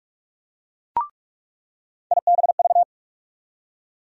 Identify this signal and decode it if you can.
Morse code, heard as a keyed tone, about 46 wpm, 700 Hz. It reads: IBV
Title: USA Call Sign Postfixes - 3 Letters 45wpm